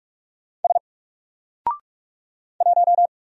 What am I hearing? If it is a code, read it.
Morse code, heard as a keyed tone, about 44 wpm, 700 Hz. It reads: S 1
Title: Single Character (Letter:Number:Punctuation:Prosign) Alt 45wpm